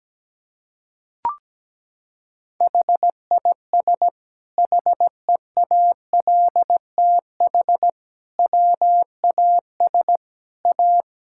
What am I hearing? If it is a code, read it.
Morse code, heard as a keyed tone, about 17 wpm, 700 Hz. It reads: HIS HEALTH WAS A
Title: Sentences from Top 1600 Words 17wpm